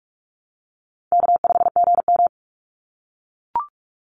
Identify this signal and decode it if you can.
Morse code, heard as a keyed tone, about 44 wpm, 700 Hz. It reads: K5GM
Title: Easy Call Signs - 45wpm